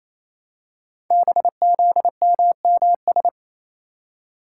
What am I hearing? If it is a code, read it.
Morse code, heard as a keyed tone, about 27 wpm, 700 Hz. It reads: BZMMS
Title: Code Groups - 5 Letters - 28wpm